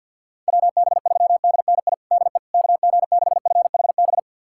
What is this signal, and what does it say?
Morse code, heard as a keyed tone, about 52 wpm, 700 Hz. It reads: WB3DNI DE KG6FHB
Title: USA Callsign DE Callsign - 6 Characters - 50wpm